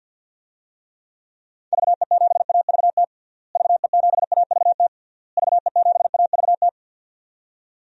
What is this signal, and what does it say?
Morse code, heard as a keyed tone, about 52 wpm, 700 Hz. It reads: VE7AVT VE7AVT VE7AVT
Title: DXCC-100 Call Signs - 6 Characters - Repeated 3x - 50wpm